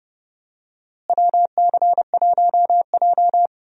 Morse code, heard as a keyed tone, about 30 wpm, 700 Hz. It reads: WC1J
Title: USA Call Signs - 4 Characters - Repeated 3x - 30wpm